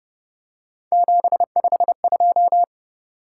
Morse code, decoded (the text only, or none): 752